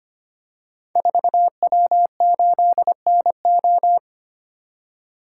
Morse code, heard as a keyed tone, about 25 wpm, 700 Hz. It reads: 4W8NO
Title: Code Groups - 5 Letter:Numbers - 25wpm